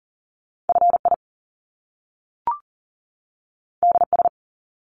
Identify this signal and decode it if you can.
Morse code, heard as a keyed tone, about 41 wpm, 700 Hz. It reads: FI DS
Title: Top Two Letter Combinations in Words 40wpm